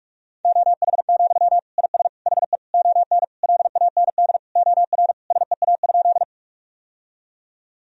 Morse code, heard as a keyed tone, about 46 wpm, 700 Hz. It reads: OH, IS HE ON LAND OR SEA?
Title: Sentences from Top 1000 Words - Review Part 1 - 45wpm